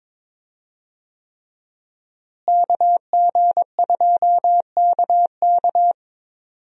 Morse code, heard as a keyed tone, about 21 wpm, 700 Hz. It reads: KG2KK
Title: USA Call Signs - 5 Characters - Repeated 3x - 22wpm